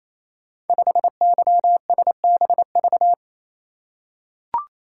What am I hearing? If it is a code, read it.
Morse code, heard as a keyed tone, about 28 wpm, 700 Hz. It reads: 5YSBV
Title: Code Groups - 5 Letter:Numbers - 28wpm